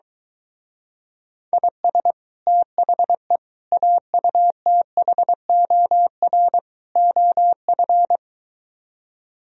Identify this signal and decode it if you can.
Morse code, heard as a keyed tone, about 22 wpm, 700 Hz. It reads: IS THE AUTHOR OF